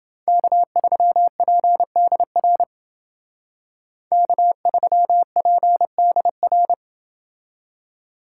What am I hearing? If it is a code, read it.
Morse code, heard as a keyed tone, about 28 wpm, 700 Hz. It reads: K3PDR K3PDR